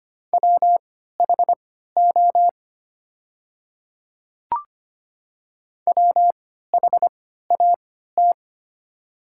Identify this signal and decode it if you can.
Morse code, heard as a keyed tone, about 25 wpm, spaced as for 14 wpm, 700 Hz. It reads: WHO WHAT
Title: 3x Character Spacing - Top 100 Words - 25wpm